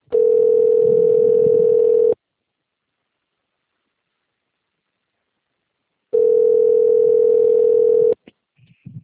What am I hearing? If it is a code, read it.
Ringback tone (North America), 440 + 480 Hz.
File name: rg-1600-unknown- - - .53024